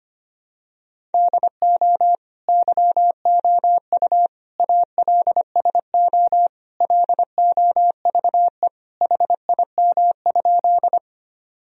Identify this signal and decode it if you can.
Morse code, heard as a keyed tone, about 25 wpm, 700 Hz. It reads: DO YOU ALSO LOVE HIM?
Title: Sentences from Top 1000 Words - Review Part 1 - 25wpm